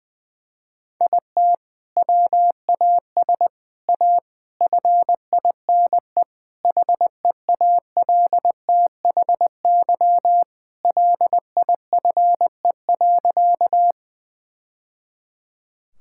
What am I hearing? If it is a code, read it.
Morse code, heard as a keyed tone, about 21 wpm, 700 Hz. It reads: IT WAS A FINE HEALTHY LIFE.
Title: Sentences from Top 1000 Words - Review Part 5 - 20wpm